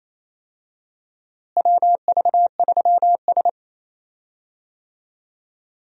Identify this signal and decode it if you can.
Morse code, heard as a keyed tone, about 28 wpm, 700 Hz. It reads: WV3S
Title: USA Call Signs - 4 Characters - Repeated 3x - 28wpm